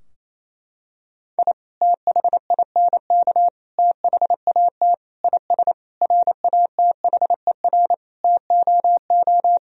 Morse code, heard as a keyed tone, about 28 wpm, 700 Hz. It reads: I THINK THAT IS RATHER TOO
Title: Sentences from Top 1000 Words - Review Part 4 - 28wpm